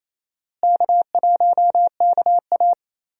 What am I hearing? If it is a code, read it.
Morse code, heard as a keyed tone, about 28 wpm, 700 Hz. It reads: K1KA
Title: USA Call Signs - 4 Characters - Repeated 2x - 28wpm